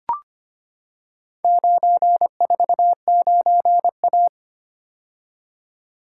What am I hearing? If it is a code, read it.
Morse code, heard as a keyed tone, about 25 wpm, 700 Hz. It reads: 949A
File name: practice / 25wpm / Four Letter-Number Combinations - 25wpm